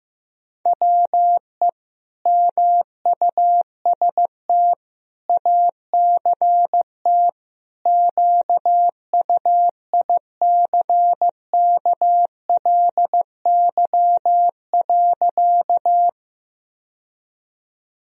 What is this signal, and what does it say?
Morse code, heard as a keyed tone, about 15 wpm, 700 Hz. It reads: WE MUST ACT QUICKLY.
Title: Sentences from Top 1000 Words - Review Part 1 - 15wpm